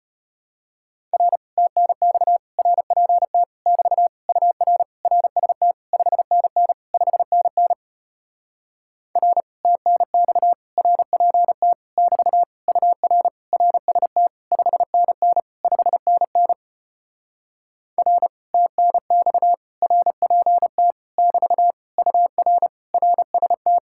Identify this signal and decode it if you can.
Morse code, heard as a keyed tone, about 33 wpm, 700 Hz. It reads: R TNX RPT <BT> UR RST 5NN 5NN R TNX RPT <BT> UR RST 5NN 5NN R TNX RPT <BT> UR RST